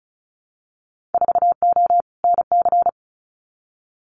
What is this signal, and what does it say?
Morse code, heard as a keyed tone, about 36 wpm, 690 Hz. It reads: 4O NC
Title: Winter Field Day Exchanges 35wpm